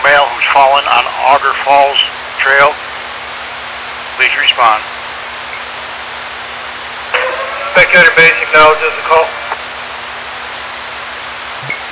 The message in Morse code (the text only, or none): none